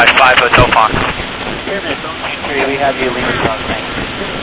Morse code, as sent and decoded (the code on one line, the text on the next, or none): none